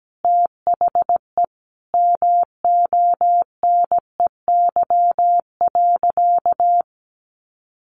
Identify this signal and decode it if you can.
Morse code, heard as a keyed tone, about 17 wpm, 700 Hz. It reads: THE MONEY.
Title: Sentences from Top 1100 Words 17wpm